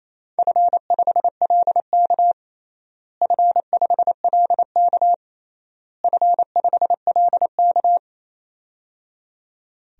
Morse code, heard as a keyed tone, about 27 wpm, 700 Hz. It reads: F5LK F5LK F5LK
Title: DXCC-100 Call Signs - 4 Characters - Repeated 3x - 28wpm